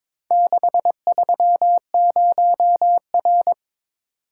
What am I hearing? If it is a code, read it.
Morse code, heard as a keyed tone, about 22 wpm, 700 Hz. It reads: 630R
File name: practice / 22wpm / Four Letter-Number Combinations - 22wpm